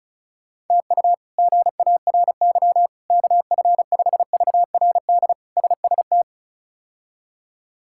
Morse code, heard as a keyed tone, about 36 wpm, 700 Hz. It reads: TU GARY KF5VRD SST
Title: K1USN SST QSO Practice - 35wpm